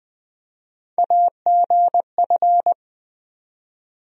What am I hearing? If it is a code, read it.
Morse code, heard as a keyed tone, about 20 wpm, 700 Hz. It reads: AGF
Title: USA Call Sign Postfixes - 3 Letters 20wpm